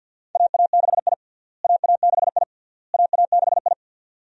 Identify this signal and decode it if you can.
Morse code, heard as a keyed tone, about 46 wpm, 700 Hz. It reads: AA6I AA6I AA6I
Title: USA Call Signs - 4 Characters - Repeated 3x - 50wpm